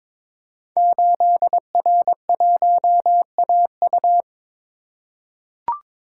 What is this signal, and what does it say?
Morse code, heard as a keyed tone, about 22 wpm, 700 Hz. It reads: 8R1AU
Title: Code Groups - 5 Letter:Numbers - 22wpm